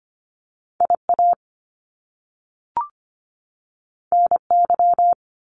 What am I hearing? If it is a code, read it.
Morse code, heard as a keyed tone, about 25 wpm, 700 Hz. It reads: IA NY
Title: US States 25wpm